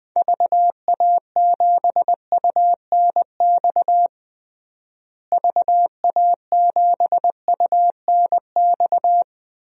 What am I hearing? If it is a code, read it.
Morse code, heard as a keyed tone, about 21 wpm, 700 Hz. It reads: VA7UNX VA7UNX
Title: DXCC-100 Call Signs - 6 Characters - Repeated 2x - 20wpm